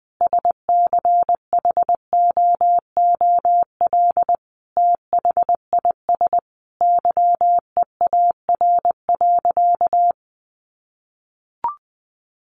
Morse code, heard as a keyed tone, about 20 wpm, 700 Hz. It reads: SCHOOL THIS YEAR.
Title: Sentences from Top 1400 Words 20wpm